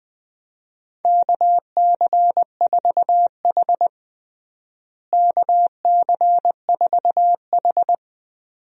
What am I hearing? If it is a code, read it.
Morse code, heard as a keyed tone, about 20 wpm, 700 Hz. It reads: KC4H KC4H
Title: USA Call Signs - 4 Characters - Repeated 2x - 20wpm